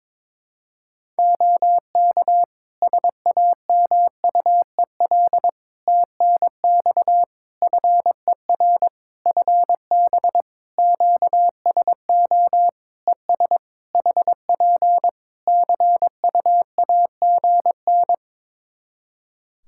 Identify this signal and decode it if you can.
Morse code, heard as a keyed tone, about 22 wpm, 700 Hz. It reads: OK SAMUEL TNX FER FB QSO ES HP CUAGN
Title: K1USN SST QSO Practice 22wpm